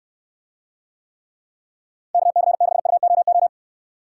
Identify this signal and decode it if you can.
Morse code, heard as a keyed tone, about 69 wpm, 700 Hz. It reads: KC6UOO